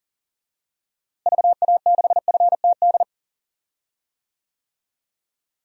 Morse code, heard as a keyed tone, about 40 wpm, 700 Hz. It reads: VA6FTD
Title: DXCC-100 Call Signs - 6 Characters - Repeated 3x - 40wpm